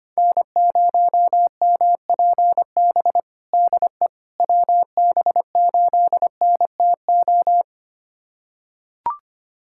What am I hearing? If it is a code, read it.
Morse code, heard as a keyed tone, about 25 wpm, 700 Hz. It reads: N0MPB DE WB8NTO
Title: USA Callsign DE Callsign 25wpm